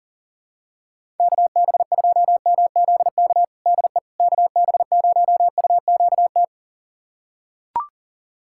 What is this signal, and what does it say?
Morse code, heard as a keyed tone, about 40 wpm, 700 Hz. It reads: KB2MZK DE KB0UQT
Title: USA Callsign DE Callsign 40wpm